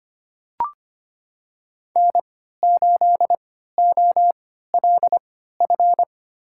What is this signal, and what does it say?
Morse code, heard as a keyed tone, about 25 wpm, spaced as for 14 wpm, 700 Hz. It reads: N8OLF
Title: USA Call Signs - 5 Characters - 3x Character Spacing - 25wpm